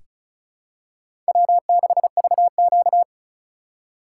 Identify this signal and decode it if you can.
Morse code, heard as a keyed tone, about 34 wpm, 700 Hz. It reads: W6VQ
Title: USA Call Signs - 4 Characters - 2x Character Spacing - 35wpm